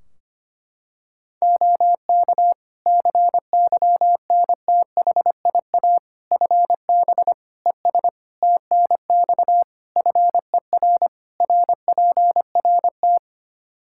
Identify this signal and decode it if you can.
Morse code, heard as a keyed tone, about 25 wpm, 700 Hz. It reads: OK CYNTHIA FB ES TNX FER RPRT
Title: K1USN SST QSO Practice 25wpm